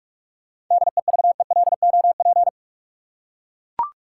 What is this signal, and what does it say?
Morse code, heard as a keyed tone, about 44 wpm, 700 Hz. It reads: DEVELOP
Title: Seven Letter Words - 45wpm